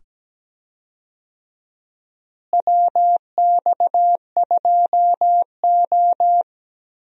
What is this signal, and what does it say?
Morse code, heard as a keyed tone, about 17 wpm, 700 Hz. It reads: WX2O
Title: USA Call Signs - 4 Characters - Repeated 2x - 17wpm